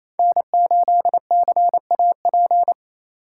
Morse code, heard as a keyed tone, about 27 wpm, 700 Hz. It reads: N8CAP
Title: USA Call Signs - 2x Character Spacing - 28wpm